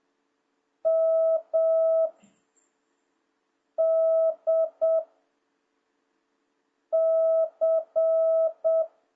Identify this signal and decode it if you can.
Morse code, heard as a keyed tone, about 7 wpm, 660 Hz. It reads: M D C